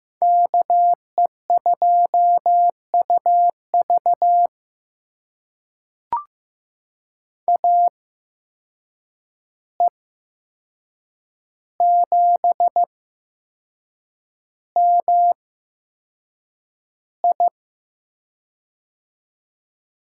Morse code, heard as a keyed tone, about 15 wpm, spaced as for 6 wpm, 700 Hz. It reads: KE2UV AE7MI
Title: USA Call Signs - 5 Characters - 8x Character Spacing - 15wpm